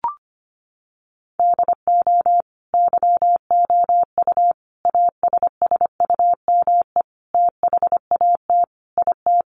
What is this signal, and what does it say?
Morse code, heard as a keyed tone, about 25 wpm, 700 Hz. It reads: DO YOU ASSUME THAT IT
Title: Sentences from Top 1500 Words 25wpm